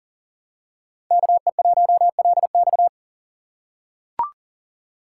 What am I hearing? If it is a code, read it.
Morse code, heard as a keyed tone, about 40 wpm, 700 Hz. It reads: KE1LX
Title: USA Call Signs - 5 Characters - 3x Character Spacing - 40wpm